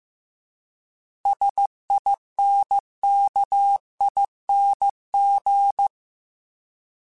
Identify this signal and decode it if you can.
Morse code, heard as a keyed tone, about 15 wpm, 800 Hz. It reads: SINKING